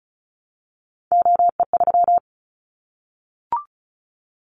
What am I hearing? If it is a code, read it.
Morse code, heard as a keyed tone, about 34 wpm, 700 Hz. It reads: OE3
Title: Callsign Prefixes for Top 25 DXCC - 35wpm